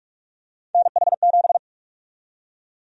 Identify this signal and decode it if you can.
Morse code, heard as a keyed tone, about 45 wpm, 700 Hz. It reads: NH7
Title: USA Call Sign Prefixes - 45wpm